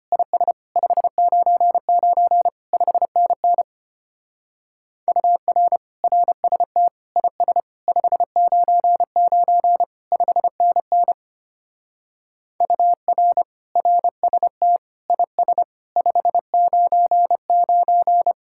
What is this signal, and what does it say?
Morse code, heard as a keyed tone, about 29 wpm, 700 Hz. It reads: IS 599 5NN UR RST IS 599 5NN UR RST IS 599
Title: Speed-Racing Common QSO Phrases - 25wpm